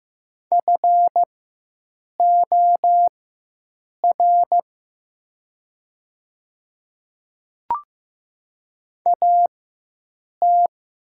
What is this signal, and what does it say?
Morse code, heard as a keyed tone, about 15 wpm, spaced as for 7 wpm, 700 Hz. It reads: FOR AT
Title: 4x Character Spacing - Top 100 Words - 15wpm